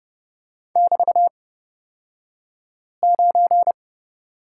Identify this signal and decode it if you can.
Morse code, heard as a keyed tone, about 30 wpm, spaced as for 5 wpm, 700 Hz. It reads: <BT>9
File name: Lesson 36d - BT - Two Characters 30wpm